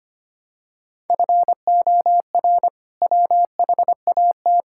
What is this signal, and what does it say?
Morse code, heard as a keyed tone, about 25 wpm, 700 Hz. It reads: FOR WHAT
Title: Sentences from Top 1000 Words - Review Part 5 - 25wpm